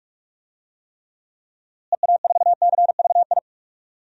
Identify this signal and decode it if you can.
Morse code, heard as a keyed tone, about 46 wpm, 700 Hz. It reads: EA4CVI
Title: DXCC-100 Call Signs - 6 Characters - Repeated 2x - 45wpm